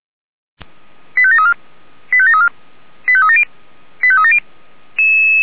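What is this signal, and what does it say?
Five-tone selective call (ZVEI), heard as quick runs of five tones, tones of about 80 ms. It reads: 85734 85734 85370 85370
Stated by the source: ZVEI five-tone call 85751 received by SDR, archived by Feuerwehr Orlamünde (fire brigade).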